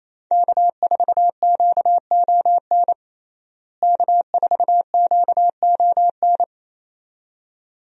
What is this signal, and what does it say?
Morse code, heard as a keyed tone, about 28 wpm, 700 Hz. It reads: K4QON K4QON
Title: USA Call Signs - 5 Characters - Repeated 2x - 28wpm